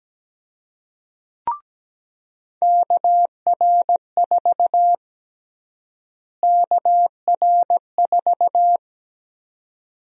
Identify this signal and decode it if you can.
Morse code, heard as a keyed tone, about 17 wpm, 700 Hz. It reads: KR4 KR4
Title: USA Call Sign Prefixes - Repeated 2x - 17wpm